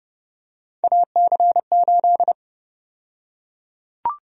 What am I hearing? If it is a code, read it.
Morse code, heard as a keyed tone, about 30 wpm, 700 Hz. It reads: AC8
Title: USA Call Sign Prefixes - 30wpm